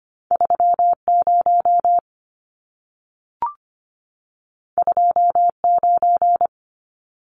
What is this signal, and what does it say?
Morse code, heard as a keyed tone, about 25 wpm, 700 Hz. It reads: 30 29
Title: Numbers - 2-Digit 25wpm